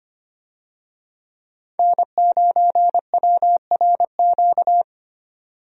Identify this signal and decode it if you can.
Morse code, heard as a keyed tone, about 25 wpm, 700 Hz. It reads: N9WRQ
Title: USA Call Signs - 5 Characters - 25wpm